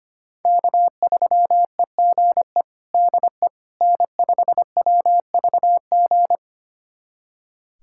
Morse code, heard as a keyed tone, about 25 wpm, 700 Hz. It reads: K3EGE DE N5WVG
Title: USA Callsign DE Callsign - 5 Characters - 25wpm